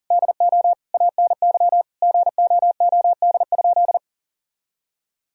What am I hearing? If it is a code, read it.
Morse code, heard as a keyed tone, about 40 wpm, 700 Hz. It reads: DO ANY GOOD?
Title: Sentences from Top 1000 Words - Review Part 4 - 40wpm